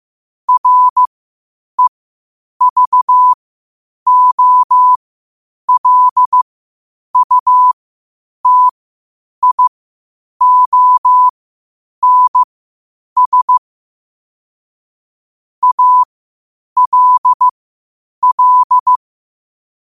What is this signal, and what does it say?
Morse code, heard as a keyed tone, about 15 wpm, spaced as for 8 wpm, 1000 Hz. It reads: REVOLUTIONS ALL
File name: qotdfiles / 8wpm